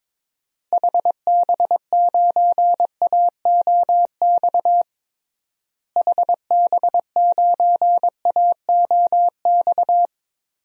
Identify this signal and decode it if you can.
Morse code, heard as a keyed tone, about 21 wpm, 700 Hz. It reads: HB9AOX HB9AOX
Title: DXCC-100 Call Signs - 6 Characters - Repeated 2x - 22wpm